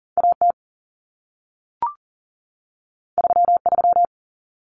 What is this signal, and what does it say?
Morse code, heard as a keyed tone, about 40 wpm, 700 Hz. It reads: AT 33
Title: Common Cut Numbers - 2-Digit 40wpm